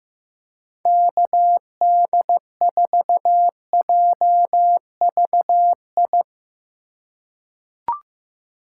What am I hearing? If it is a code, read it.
Morse code, heard as a keyed tone, about 15 wpm, 700 Hz. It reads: KD4JVI
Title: USA Call Signs - 6 Characters - 4x Character Spacing - 15wpm